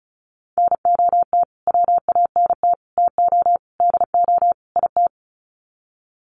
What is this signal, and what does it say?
Morse code, heard as a keyed tone, about 34 wpm, 700 Hz. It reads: NOT WANT TO DO IT
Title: Sets of 5 Words Encore 35wpm